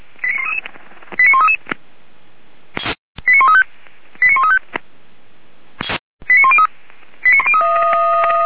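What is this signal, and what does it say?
Five-tone selective call (ZVEI), heard as quick runs of five tones, tones of about 70 ms. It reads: 89133 89135 89135 89103
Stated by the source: ZVEI five-tone call received by SDR, archived by the fire brigade Feuerwehr Orlamünde.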